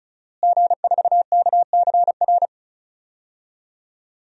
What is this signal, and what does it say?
Morse code, heard as a keyed tone, about 36 wpm, 700 Hz. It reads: G4KCR
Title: DXCC-100 Call Signs - 5 Characters - Repeated 2x - 35wpm